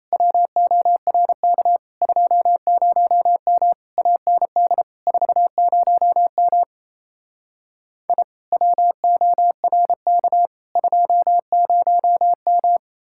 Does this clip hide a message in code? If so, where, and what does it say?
Yes: Morse code from 0.1 to 12.8 s, heard as a keyed tone, about 37 wpm, 700 Hz. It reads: WORK 20M AND 40M I WORK 20M